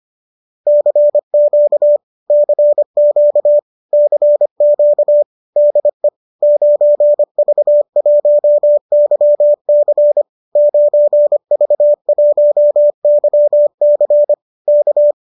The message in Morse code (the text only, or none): CQ CQ CQ DE 9V1YC 9V1YC K